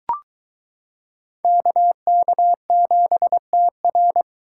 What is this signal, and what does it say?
Morse code, heard as a keyed tone, about 23 wpm, 700 Hz. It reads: KK7TR